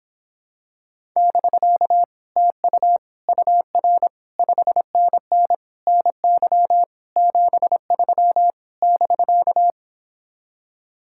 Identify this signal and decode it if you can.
Morse code, heard as a keyed tone, about 26 wpm, 700 Hz. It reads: <BK> TU UR 5NN NY 73 <BK>